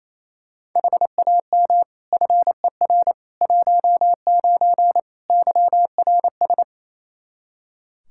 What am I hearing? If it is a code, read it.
Morse code, heard as a keyed tone, about 28 wpm, 700 Hz. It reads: HAM FER 19 YRS
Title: K1USN SST QSO Practice 28wpm